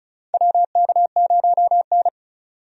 Morse code, heard as a keyed tone, about 34 wpm, 700 Hz. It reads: WK0N
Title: USA Call Signs - 4 Characters - 35wpm